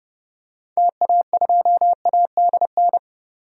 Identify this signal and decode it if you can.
Morse code, heard as a keyed tone, about 30 wpm, 700 Hz. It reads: TA2ADN
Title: DXCC-100 Call Signs - 6 Characters - 30wpm